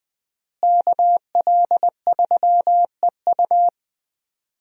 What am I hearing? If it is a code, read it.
Morse code, heard as a keyed tone, about 21 wpm, 700 Hz. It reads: KL3EU
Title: USA Call Signs - 5 Characters - Repeated 3x - 20wpm